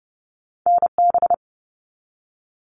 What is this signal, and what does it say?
Morse code, heard as a keyed tone, about 30 wpm, 700 Hz. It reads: NB
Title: Canadian Provinces and Territories 30wpm